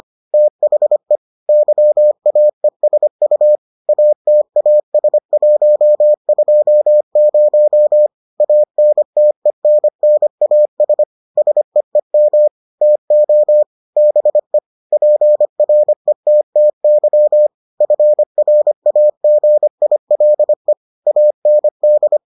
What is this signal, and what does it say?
Morse code, heard as a keyed tone, about 25 wpm, 600 Hz. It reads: THE YAESU ATAS120 ANTENNAS SEEM TO BE PRETTY FRAGILE AND